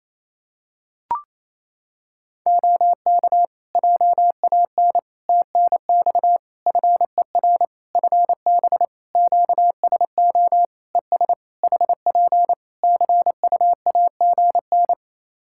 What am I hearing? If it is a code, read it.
Morse code, heard as a keyed tone, about 28 wpm, 700 Hz. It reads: OK JAN TNX FER FB QSO ES HP CUAGN
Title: K1USN SST QSO Practice 28wpm